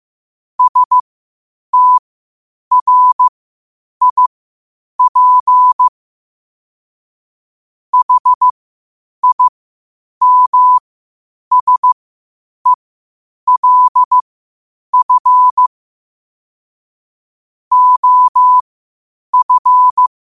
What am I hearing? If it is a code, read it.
Morse code, heard as a keyed tone, about 15 wpm, spaced as for 8 wpm, 1000 Hz. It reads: STRIP HIMSELF OF
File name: qotdfiles / 8wpm